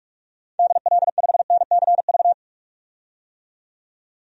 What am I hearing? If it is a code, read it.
Morse code, heard as a keyed tone, about 42 wpm, 700 Hz. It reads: DL5NCV
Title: DXCC-100 Call Signs - 6 Characters - 45wpm